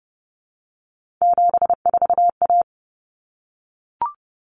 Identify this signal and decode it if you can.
Morse code, heard as a keyed tone, about 30 wpm, 700 Hz. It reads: 74A
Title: Common Cut Numbers - 3-Digit 30wpm